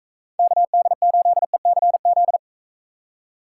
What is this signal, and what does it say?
Morse code, heard as a keyed tone, about 45 wpm, 700 Hz. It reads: KD8ECZ